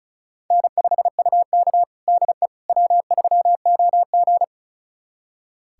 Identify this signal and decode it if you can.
Morse code, heard as a keyed tone, about 36 wpm, 700 Hz. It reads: N5UK DE W3OG
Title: USA Callsign DE Callsign - 4 Characters - 35wpm